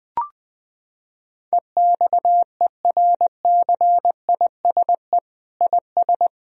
Morse code, heard as a keyed tone, about 20 wpm, 700 Hz. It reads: EXERCISE IS
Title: Sentences from Top 1000 Words - Review Part 4 - 20wpm